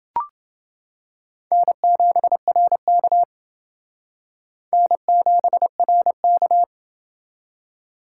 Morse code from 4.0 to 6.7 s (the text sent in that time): N7RK